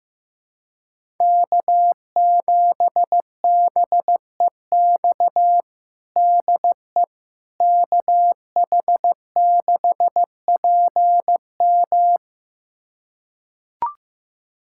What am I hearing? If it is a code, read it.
Morse code, heard as a keyed tone, about 15 wpm, 700 Hz. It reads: K7BEX DE KH6PM
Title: USA Callsign DE Callsign - 5 Characters - 15wpm